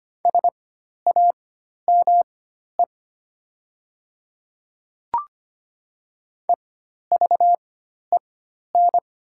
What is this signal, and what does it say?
Morse code, heard as a keyed tone, about 25 wpm, spaced as for 12 wpm, 700 Hz. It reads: SAME EVEN
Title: 4x Character Spacing - Top 200 Words - 25wpm